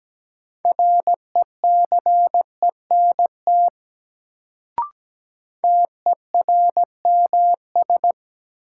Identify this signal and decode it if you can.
Morse code, heard as a keyed tone, about 17 wpm, 700 Hz. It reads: RECENT TERMS
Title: Sets of 3 Words - New Word Drill - 17wpm